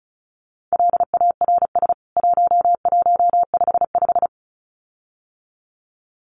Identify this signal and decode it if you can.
Morse code, heard as a keyed tone, about 34 wpm, 700 Hz. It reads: LARS 1155
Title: CWT Exchange 35wpm